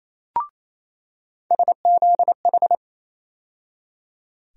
Morse code, heard as a keyed tone, about 28 wpm, 700 Hz. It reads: SZH
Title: USA Call Sign Postfixes - 3 Letters 28wpm